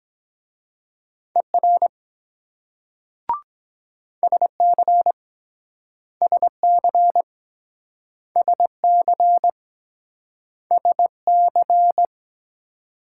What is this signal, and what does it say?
Morse code, heard as a keyed tone, about 23 wpm, 700 Hz. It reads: ER SC SC SC SC